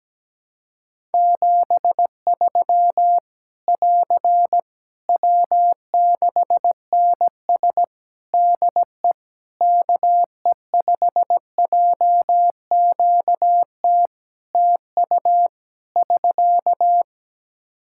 Morse code, heard as a keyed tone, about 17 wpm, 700 Hz. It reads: 73 <AR> W6NS DE KE5JQT TU <SK>
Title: Practice QSO - Exchange 4 - Ending - 17wpm